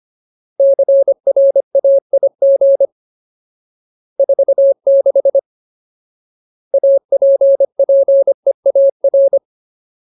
Morse code, heard as a keyed tone, about 25 wpm, 550 Hz. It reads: CRAIG 46 APPEAR
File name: Ultimate Word List / 25WPM Ultimate